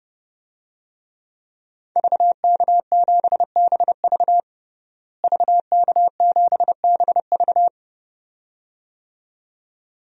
Morse code, heard as a keyed tone, about 32 wpm, 700 Hz. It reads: VK7BV VK7BV
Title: DXCC-100 Call Signs - 5 Characters - Repeated 2x - 30wpm